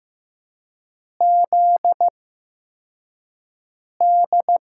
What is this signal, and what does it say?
Morse code, heard as a keyed tone, about 15 wpm, spaced as for 4 wpm, 700 Hz. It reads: ZD
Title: USA Call Signs - 5 Characters - 8x Character Spacing - 15wpm